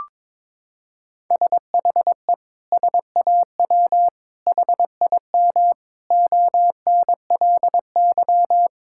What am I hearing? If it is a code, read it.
Morse code, heard as a keyed tone, about 22 wpm, 700 Hz. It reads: SHE SAW HIM ONLY
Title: Sentences from Top 1600 Words 22wpm